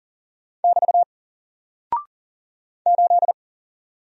Morse code, heard as a keyed tone, about 40 wpm, 700 Hz. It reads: <BT> 8
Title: Single Character (Letter:Number:Punctuation:Prosign) 40wpm